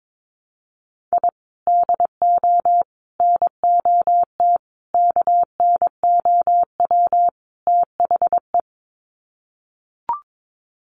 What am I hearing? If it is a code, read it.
Morse code, heard as a keyed tone, about 22 wpm, 700 Hz. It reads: I DO NOT KNOW THE
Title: Sets of 5 Words 22wpm